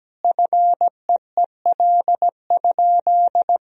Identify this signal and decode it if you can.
Morse code, heard as a keyed tone, about 17 wpm, 700 Hz. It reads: FEEL?
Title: Sentences from Top 400 Words and Abbreviations 17wpm